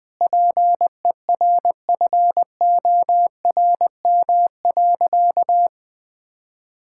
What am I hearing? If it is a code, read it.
Morse code, heard as a keyed tone, about 20 wpm, 700 Hz. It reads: PERFORM.
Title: Sentences from Top 1000 Words - Review Part 5 - 20wpm